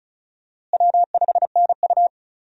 Morse code, heard as a keyed tone, about 34 wpm, 700 Hz. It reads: W5NU
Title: USA Call Signs - 4 Characters - Repeated 2x - 35wpm